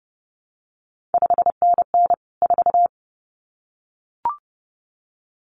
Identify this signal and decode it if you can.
Morse code, heard as a keyed tone, about 30 wpm, 700 Hz. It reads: 5NN 4
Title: CQ WW DX Contest Exchange - 30wpm